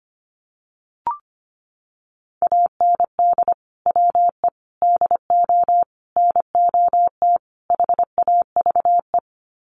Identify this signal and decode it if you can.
Morse code, heard as a keyed tone, about 25 wpm, 700 Hz. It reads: AND WE DO NOT HAVE
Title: Sets of 5 Words Encore 25wpm